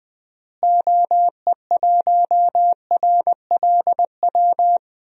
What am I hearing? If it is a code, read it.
Morse code, heard as a keyed tone, about 20 wpm, 700 Hz. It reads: OE1RLW
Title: DXCC-100 Call Signs - 6 Characters - Repeated 3x - 20wpm